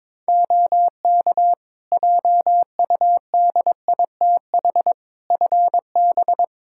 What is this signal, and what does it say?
Morse code, heard as a keyed tone, about 21 wpm, 700 Hz. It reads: OK JUDITH FB
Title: Practice QSO - Exchange 2 - Info - 22wpm